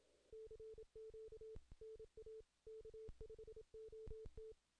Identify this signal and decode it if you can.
Morse code, heard as a keyed tone, about 27 wpm, 440 Hz. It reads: CQ NA K5OT
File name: 2016 JAN CW NAQP / 2016 Jan naqp k5ot - 0-10